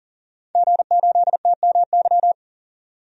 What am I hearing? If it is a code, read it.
Morse code, heard as a keyed tone, about 40 wpm, 700 Hz. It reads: G8TMY
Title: DXCC-100 Call Signs - 5 Characters - 40wpm